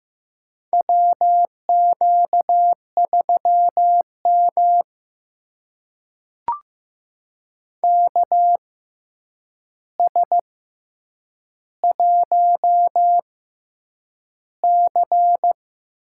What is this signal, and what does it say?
Morse code, heard as a keyed tone, about 15 wpm, spaced as for 8 wpm, 700 Hz. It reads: WQ3M KS1C
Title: USA Call Signs - 4 Characters - 6x Character Spacing - 15wpm